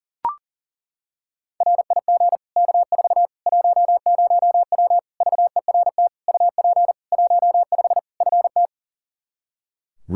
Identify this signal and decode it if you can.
Morse code, heard as a keyed tone, about 37 wpm, 700 Hz. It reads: RIG K4 10W VERT UP 15 FT
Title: Practice QSO - Exchange 2 - Info - 40wpm